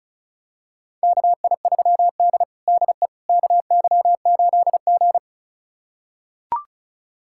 Morse code, heard as a keyed tone, about 36 wpm, 700 Hz. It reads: KI3D DE KY8G
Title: USA Callsign DE Callsign - 4 Characters - 35wpm